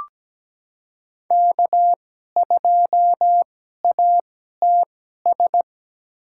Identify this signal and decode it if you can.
Morse code, heard as a keyed tone, about 17 wpm, spaced as for 12 wpm, 700 Hz. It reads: K2ATS
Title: USA Call Signs - 5 Characters - 2x Character Spacing - 17wpm